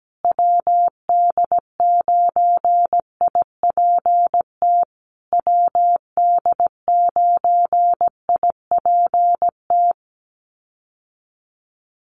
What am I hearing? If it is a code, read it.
Morse code, heard as a keyed tone, about 17 wpm, 700 Hz. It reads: WD9IPT WD9IPT
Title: USA Call Signs Repeated 3x 17wpm